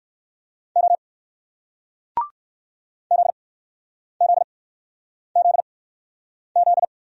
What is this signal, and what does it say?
Morse code, heard as a keyed tone, about 60 wpm, spaced as for 9 wpm, 700 Hz. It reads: O ZZZZ